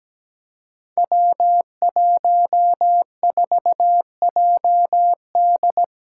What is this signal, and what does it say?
Morse code, heard as a keyed tone, about 17 wpm, 700 Hz. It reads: W14JD
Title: Five Challenging Letter-Number Combinations 17wpm